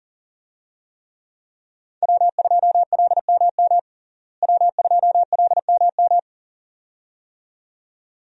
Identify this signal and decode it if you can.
Morse code, heard as a keyed tone, about 40 wpm, 700 Hz. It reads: W2LMM W2LMM
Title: USA Call Signs - 5 Characters - Repeated 2x - 40wpm